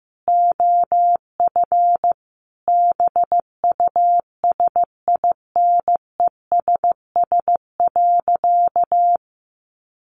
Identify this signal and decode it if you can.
Morse code, heard as a keyed tone, about 15 wpm, 700 Hz. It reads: OF BUSINESS.
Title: Sentences from Top 700 Words 15wpm